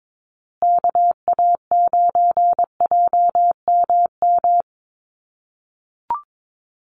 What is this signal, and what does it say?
Morse code, heard as a keyed tone, about 22 wpm, 700 Hz. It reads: KA9JMM
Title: USA Call Signs Repeated 2x 22wpm